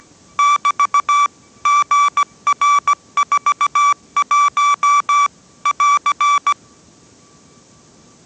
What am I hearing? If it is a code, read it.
Morse code, heard as a keyed tone, about 17 wpm, 1160 Hz. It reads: <BT> GR41 <AR>
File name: 18wpm